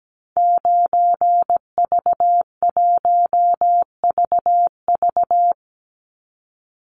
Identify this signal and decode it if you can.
Morse code, heard as a keyed tone, about 17 wpm, 700 Hz. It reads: 9V1VV
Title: Call Signs Repeated 2x - 17wpm